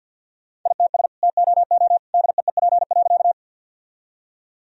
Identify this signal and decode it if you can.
Morse code, heard as a keyed tone, about 52 wpm, 700 Hz. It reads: ITS TOO DEEP.
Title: Sentences from Top 1000 Words - Review Part 3 - 50wpm